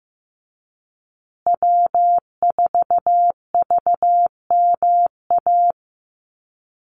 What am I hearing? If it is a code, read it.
Morse code, heard as a keyed tone, about 15 wpm, 700 Hz. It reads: W4VMA
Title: USA Call Signs Repeated 2x 15wpm